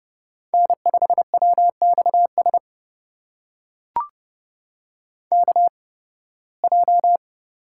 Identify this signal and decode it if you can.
Morse code, heard as a keyed tone, about 30 wpm, spaced as for 20 wpm, 700 Hz. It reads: N5WXS KJ
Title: USA Call Signs - 5 Characters - 8x Character Spacing - 30wpm